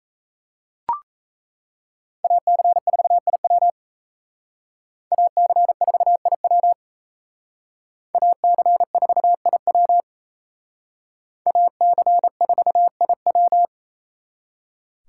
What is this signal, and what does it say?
Morse code, heard as a keyed tone, about 35 wpm, 700 Hz. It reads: AC4IW AC4IW AC4IW AC4IW
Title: Speed Racing - USA 2x2 Call Signs 28wpm